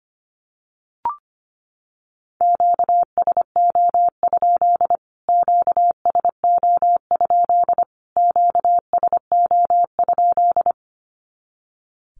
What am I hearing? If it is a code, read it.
Morse code, heard as a keyed tone, about 25 wpm, 700 Hz. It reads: QSO? QSO? QSO?
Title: Q-Codes Repeated 3x 25wpm